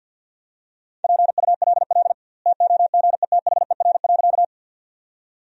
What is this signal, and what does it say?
Morse code, heard as a keyed tone, about 50 wpm, 700 Hz. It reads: PULL TOGETHER.
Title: 2x Word-Spacing - Sentences from Top 400 Words - 50wpm